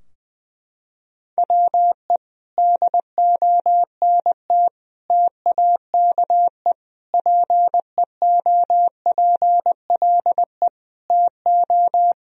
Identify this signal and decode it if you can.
Morse code, heard as a keyed tone, about 20 wpm, 700 Hz. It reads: WE DONT TAKE PEOPLE TO
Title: Sentences from Top 1000 Words - Review Part 4 - 20wpm